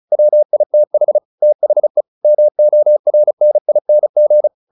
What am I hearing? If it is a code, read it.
Morse code, heard as a keyed tone, about 35 wpm, 600 Hz. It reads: WITH THE MORNING
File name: the old man and the sea / the old man and the sea 1 35wpm